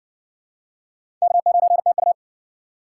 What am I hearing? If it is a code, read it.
Morse code, heard as a keyed tone, about 60 wpm, 700 Hz. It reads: K9TU